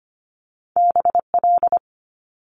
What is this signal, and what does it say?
Morse code, heard as a keyed tone, about 25 wpm, 700 Hz. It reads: BL
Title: Multi-letter Phonemes Quick Fire 25wpm